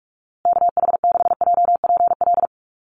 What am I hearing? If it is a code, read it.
Morse code, heard as a keyed tone, about 44 wpm, 700 Hz. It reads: KH6JPL
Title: USA Call Signs Repeated 3x 45wpm